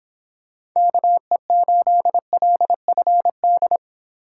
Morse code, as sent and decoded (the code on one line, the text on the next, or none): -.- . ---.. .-.. ..-. -..
KE8LFD